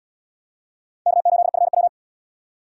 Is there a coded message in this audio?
Yes: Morse code from 1.1 to 1.9 s, heard as a keyed tone, about 69 wpm, 690 Hz. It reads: K8FW